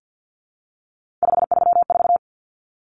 Morse code, heard as a keyed tone, about 50 wpm, 700 Hz. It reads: 534
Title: Common Cut Numbers - 3-Digit 50wpm